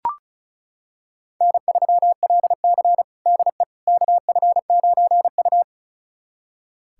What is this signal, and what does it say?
Morse code, heard as a keyed tone, about 34 wpm, 700 Hz. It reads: N3LC DE KF9U
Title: USA Callsign DE Callsign - 4 Characters - 35wpm